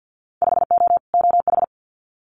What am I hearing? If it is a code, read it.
Morse code, heard as a keyed tone, about 50 wpm, 700 Hz. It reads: 5O OH
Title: Winter Field Day Exchanges Encore 50wpm